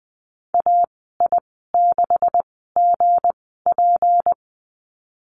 Morse code, heard as a keyed tone, about 20 wpm, spaced as for 15 wpm, 700 Hz. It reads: AI6GP
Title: 161 Lesson 26g - Letter g - Call Signs 20wpm